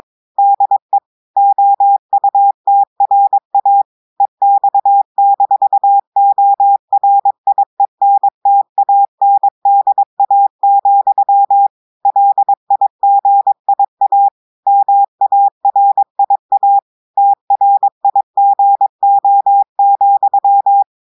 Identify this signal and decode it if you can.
Morse code, heard as a keyed tone, about 22 wpm, 800 Hz. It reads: DE OUTRA EX-ORIENTANDA, LIGIA MARIA TRIGO,
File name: audios / portugais / portugais texte en clair 1 vitesse 22 mots minutes